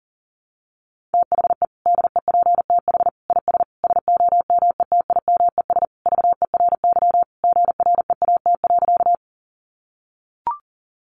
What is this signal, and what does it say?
Morse code, heard as a keyed tone, about 40 wpm, 700 Hz. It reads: THE DEPTH IS SOMETIMES VERY GREAT.
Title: Sentences from Top 1200 Words 40wpm